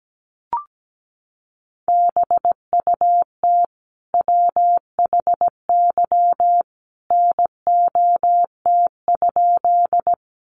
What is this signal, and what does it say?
Morse code, heard as a keyed tone, about 17 wpm, 700 Hz. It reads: BUT WHY NOT?
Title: Sentences from Top 600 Words 17wpm